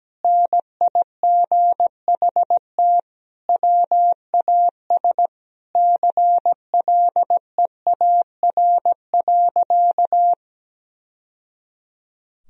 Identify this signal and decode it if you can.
Morse code, heard as a keyed tone, about 17 wpm, 700 Hz. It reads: NIGHT WAS CLEAR.
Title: Sentences from Top 1000 Words - Review Part 1 - 17wpm